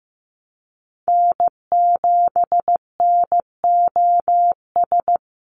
Morse code, heard as a keyed tone, about 15 wpm, 700 Hz. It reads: N7NOS
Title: USA Call Signs 15wpm